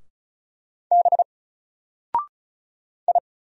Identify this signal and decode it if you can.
Morse code, heard as a keyed tone, about 36 wpm, 700 Hz. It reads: B I
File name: Single-Letter-Number-35wpm